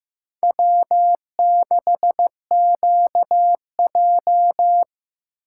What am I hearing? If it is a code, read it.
Morse code, heard as a keyed tone, about 15 wpm, 700 Hz. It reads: W6QJ
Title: USA Call Signs - 4 Characters - 6x Character Spacing - 15wpm